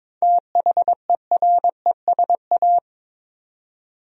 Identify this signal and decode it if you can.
Morse code, heard as a keyed tone, about 22 wpm, 700 Hz. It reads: THERESA
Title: USA Common Names - 22wpm